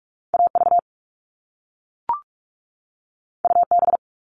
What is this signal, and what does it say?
Morse code, heard as a keyed tone, about 46 wpm, 700 Hz. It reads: AV UB
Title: All Cut Numbers - 2-Digit 45wpm